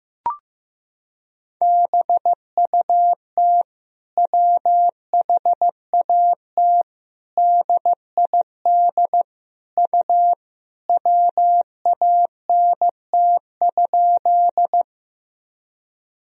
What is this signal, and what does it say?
Morse code, heard as a keyed tone, about 15 wpm, 700 Hz. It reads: BUT WHAT DID U WANT?
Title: Sentences from Top 200 Words and Abbreviations 15wpm